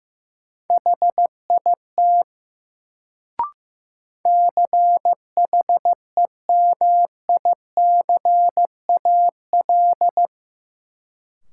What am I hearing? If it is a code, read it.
Morse code, heard as a keyed tone, about 15 wpm, 700 Hz. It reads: HIT CHEMICAL
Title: Sentences from Top 1200 Words - New Word Drill - 15wpm